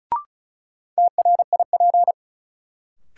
Morse code, heard as a keyed tone, about 36 wpm, 700 Hz. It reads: TRIP
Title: Sentences from Top 1000 Words - Review Part 4 - New Word Drill - 35wpm